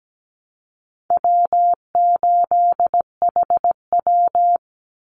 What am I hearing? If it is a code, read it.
Morse code, heard as a keyed tone, about 17 wpm, 700 Hz. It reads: W8HW
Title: Call Signs - 17wpm